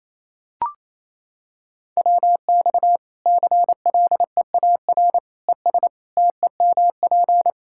Morse code, heard as a keyed tone, about 27 wpm, 700 Hz. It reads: WX CLEAR ES TEMP
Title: Practice QSO - Exchange 2 - Info - 28wpm